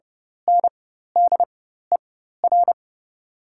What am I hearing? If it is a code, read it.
Morse code, heard as a keyed tone, about 30 wpm, spaced as for 14 wpm, 690 Hz. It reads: NDER
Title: 4x Character Spacing - Top 500 Words - 30wpm